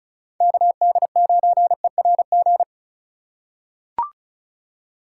Morse code, heard as a keyed tone, about 34 wpm, 700 Hz. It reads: KD9ERG
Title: USA Call Signs - 6 Characters - 2x Character Spacing - 35wpm